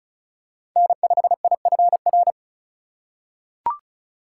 Morse code, heard as a keyed tone, about 36 wpm, 700 Hz. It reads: N5IFR
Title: USA Call Signs - 2x Character Spacing - 35wpm